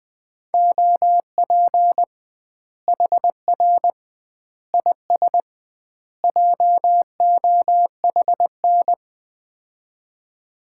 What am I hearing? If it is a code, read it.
Morse code, heard as a keyed tone, about 21 wpm, 700 Hz. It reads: OP HR IS JOHN
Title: Common QSO Phrases - 2x Word-Spacing - 20wpm